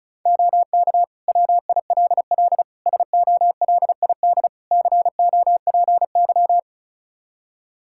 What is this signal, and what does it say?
Morse code, heard as a keyed tone, about 34 wpm, 700 Hz. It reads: OK WILL SOLID COPY
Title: Practice QSO - Exchange 3 - More Info - 35wpm